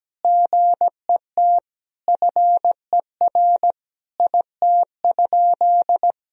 Morse code, heard as a keyed tone, about 17 wpm, 700 Hz. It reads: GET FER IT?
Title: Sentences from Top 400 Words and Abbreviations 17wpm